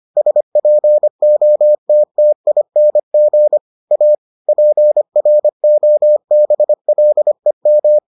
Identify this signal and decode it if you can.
Morse code, heard as a keyed tone, about 25 wpm, 600 Hz. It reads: SPOTTING A PROBLEM